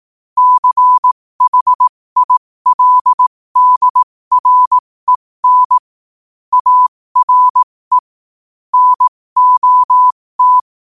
Morse code, heard as a keyed tone, about 18 wpm, spaced as for 15 wpm, 1000 Hz. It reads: CHILDREN ARE NOT
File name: qotdfiles / 15wpm